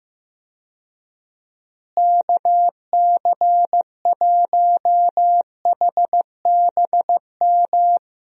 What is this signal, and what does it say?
Morse code, heard as a keyed tone, about 15 wpm, 700 Hz. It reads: KC1HBM
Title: USA Call Signs - 6 Characters - Repeated 3x - 15wpm